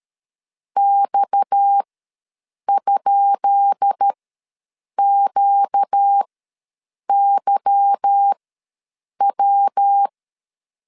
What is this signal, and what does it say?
Morse code, heard as a keyed tone, about 13 wpm, spaced as for 7 wpm, 800 Hz. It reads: X?QYW
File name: LEZIONI / 10 SIGNS / 10B INT ws 12wpm 3527